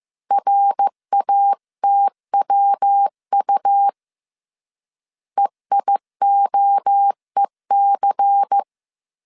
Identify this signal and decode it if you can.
Morse code, heard as a keyed tone, about 15 wpm, 800 Hz. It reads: RATWU EIOEC
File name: CW / DET / 1 6 1500C S B 14wpm 2958